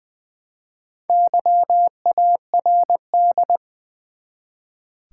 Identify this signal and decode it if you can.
Morse code, heard as a keyed tone, about 20 wpm, 700 Hz. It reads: YARD
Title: Build A Word - 1 Letter At A Time - 4-Letter Words - 20wpm